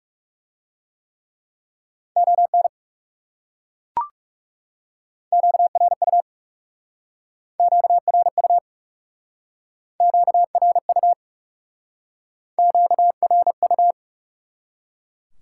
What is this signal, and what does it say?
Morse code, heard as a keyed tone, about 38 wpm, 700 Hz. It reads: ON QRU QRU QRU QRU